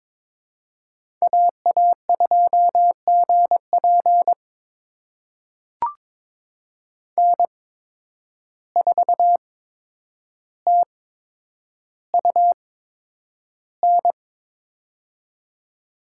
Morse code, heard as a keyed tone, about 23 wpm, spaced as for 9 wpm, 700 Hz. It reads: AA2GP N4TUN
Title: USA Call Signs - 5 Characters - 8x Character Spacing - 22wpm